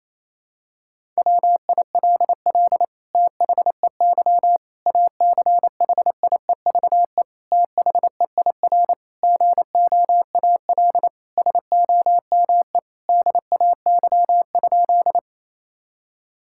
Morse code, heard as a keyed tone, about 28 wpm, 700 Hz. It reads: WILL THEY ACHIEVE THEIR GOAL SOME DAY?
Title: Sentences from Top 1000 Words - Review Part 6 - 28wpm